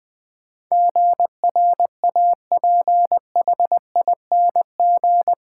Morse code, heard as a keyed tone, about 20 wpm, 700 Hz. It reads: GRAPHING
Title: Difficult Words - Speed-Racing with Hints - 20wpm